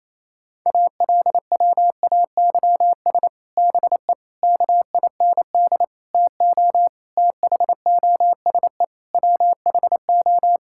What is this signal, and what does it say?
Morse code, heard as a keyed tone, about 27 wpm, 700 Hz. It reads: ALWAYS BE KIND TO THOSE WHO
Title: Sentences from Top 1600 Words 28wpm